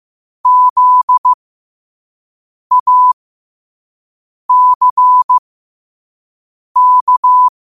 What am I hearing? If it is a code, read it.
Morse code, heard as a keyed tone, about 15 wpm, spaced as for 5 wpm, 1000 Hz. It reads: ZACK